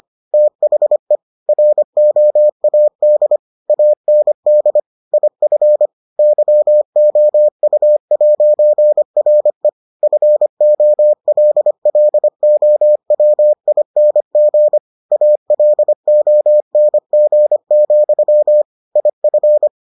Morse code, heard as a keyed tone, about 25 wpm, 600 Hz. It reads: THE ROAD AND IF YOU'RE FOLLOWING ALONG, IF